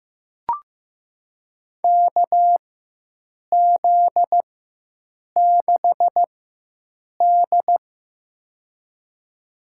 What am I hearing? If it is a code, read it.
Morse code, heard as a keyed tone, about 15 wpm, spaced as for 7 wpm, 700 Hz. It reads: KZ6D
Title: USA Call Signs - 4 Characters - 4x Character Spacing - 15wpm